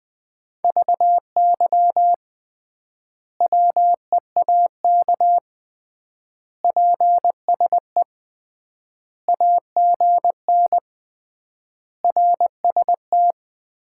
Morse code, heard as a keyed tone, about 20 wpm, 700 Hz. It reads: VY WEAK PSE AGN RST
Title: Common QSO Phrases - 3x Word-Spacing - 20wpm